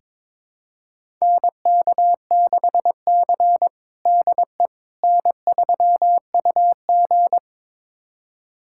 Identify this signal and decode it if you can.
Morse code, heard as a keyed tone, about 22 wpm, 700 Hz. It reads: NK6C DE N3UG
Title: USA Callsign DE Callsign - 4 Characters - 22wpm